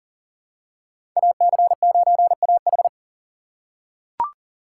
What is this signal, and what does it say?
Morse code, heard as a keyed tone, about 40 wpm, 700 Hz. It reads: AC9AH
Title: USA Call Signs - 5 Characters - 3x Character Spacing - 40wpm